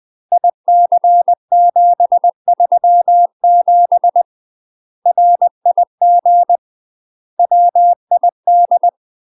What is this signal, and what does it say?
Morse code, heard as a keyed tone, about 20 wpm, 700 Hz. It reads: IC737 RIG WID
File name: UK MASTER QSOs TRAINING / 20WPM / UK MASTER QSOs 0010